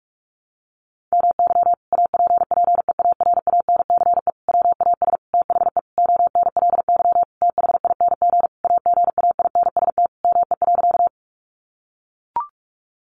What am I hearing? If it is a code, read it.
Morse code, heard as a keyed tone, about 44 wpm, 700 Hz. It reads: MY APPEARANCE WAS THE ONLY THING AGAINST ME.
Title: Sentences from Top 1100 Words 45wpm